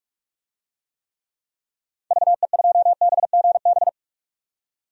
Morse code, heard as a keyed tone, about 46 wpm, 700 Hz. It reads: VE2BGB
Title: DXCC-100 Call Signs - 6 Characters - 45wpm